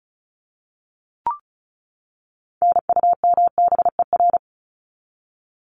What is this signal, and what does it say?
Morse code, heard as a keyed tone, about 34 wpm, 700 Hz. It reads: NUMBER
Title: ES:S Suffix - Top 100 Words with Base Words 35wpm